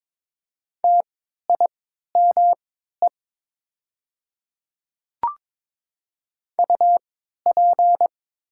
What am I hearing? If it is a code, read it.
Morse code, heard as a keyed tone, about 21 wpm, spaced as for 12 wpm, 700 Hz. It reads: TIME UP
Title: 3x Character Spacing - Top 100 Words - 22wpm